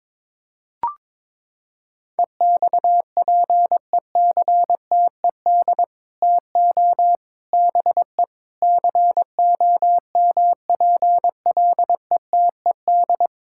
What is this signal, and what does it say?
Morse code, heard as a keyed tone, about 22 wpm, 700 Hz. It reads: EXPECTED TO BE COMPLETED
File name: practice / 22wpm / Sets of 8 Words - 22wpm